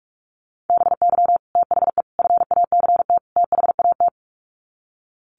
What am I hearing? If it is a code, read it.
Morse code, heard as a keyed tone, about 44 wpm, 700 Hz. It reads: BY THE FACT THAT
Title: Sets of 4 Words 45wpm